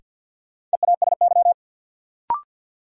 Morse code, heard as a keyed tone, about 48 wpm, 700 Hz. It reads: EASY
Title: Sentences from Top 700 Words - New Word Drill - 50wpm